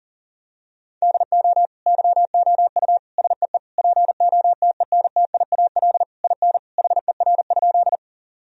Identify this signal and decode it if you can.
Morse code, heard as a keyed tone, about 40 wpm, 700 Hz. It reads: DO YOU SEE POTENTIAL IN HER?
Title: Sentences from Top 1000 Words - Review Part 4 - 40wpm